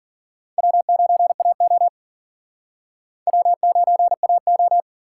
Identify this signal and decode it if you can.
Morse code, heard as a keyed tone, about 51 wpm, 700 Hz. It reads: W9AO W9AO